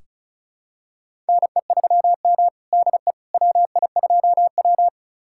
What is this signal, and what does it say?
Morse code, heard as a keyed tone, about 36 wpm, 700 Hz. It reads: NE3M DE WI2W
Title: USA Callsign DE Callsign - 4 Characters - 35wpm